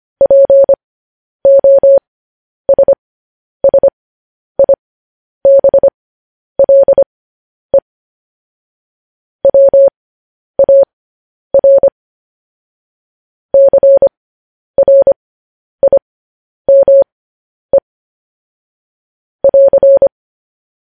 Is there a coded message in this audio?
Yes: Morse code from 0.2 to 20.1 s, heard as a keyed tone, about 25 wpm, spaced as for 10 wpm, 550 Hz. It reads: POSSIBLE WAR CRIME <AR>